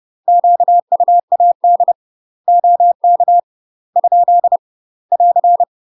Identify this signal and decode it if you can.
Morse code, heard as a keyed tone, about 30 wpm, 700 Hz. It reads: QUAD OK ? <AR>
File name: UK MASTER QSOs TRAINING / 30WPM / UK MASTER QSOs 0047